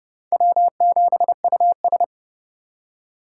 Morse code, heard as a keyed tone, about 30 wpm, 700 Hz. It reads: W7US
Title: USA Call Signs - 4 Characters - 4x Character Spacing - 30wpm